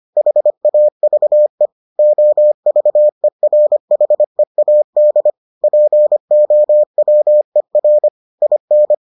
Morse code, heard as a keyed tone, about 25 wpm, 600 Hz. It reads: HAVE OVERHEAD POWER IN